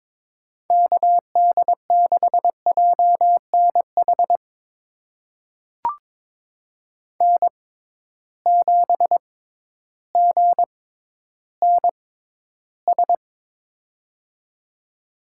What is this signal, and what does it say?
Morse code, heard as a keyed tone, about 22 wpm, spaced as for 12 wpm, 700 Hz. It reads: KD6JNH N7GNS
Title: USA Call Signs - 6x Character Spacing - 22wpm